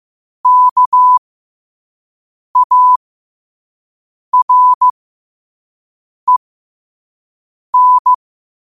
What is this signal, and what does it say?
Morse code, heard as a keyed tone, about 15 wpm, spaced as for 5 wpm, 1000 Hz. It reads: KAREN
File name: qotdfiles / 5wpm